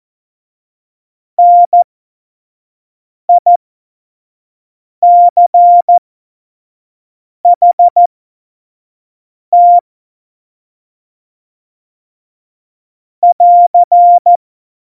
Morse code, heard as a keyed tone, about 14 wpm, spaced as for 5 wpm, 700 Hz. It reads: NICHT <AR>
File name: Morsekurs / Witze / Witz 27 March 2018, 25 F70